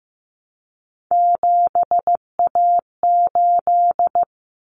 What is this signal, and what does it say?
Morse code, heard as a keyed tone, about 15 wpm, 700 Hz. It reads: 7A8
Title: Common Cut Numbers - 3-Digit 15wpm